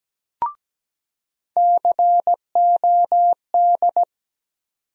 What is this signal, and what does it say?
Morse code, heard as a keyed tone, about 17 wpm, 700 Hz. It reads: COD
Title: USA Call Sign Postfixes - 3 Letters 17wpm